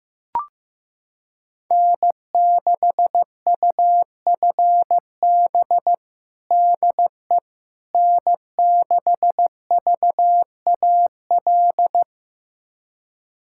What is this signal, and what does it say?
Morse code, heard as a keyed tone, about 15 wpm, 700 Hz. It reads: N6UFB DE N6VAL
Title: USA Callsign DE Callsign - 5 Characters - 15wpm